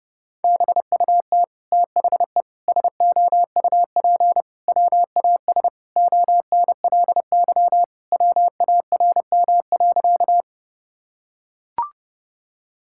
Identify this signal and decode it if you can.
Morse code, heard as a keyed tone, about 30 wpm, 700 Hz. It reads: BUT THE SOUP WAS ONLY WARM.
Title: Sentences from Top 1000 Words - Review Part 6 - 30wpm